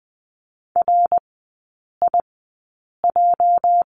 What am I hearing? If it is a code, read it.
Morse code, heard as a keyed tone, about 20 wpm, spaced as for 8 wpm, 700 Hz. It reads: RIJ
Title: 205 Lesson 33e - Letter j - Three Characters 20wpm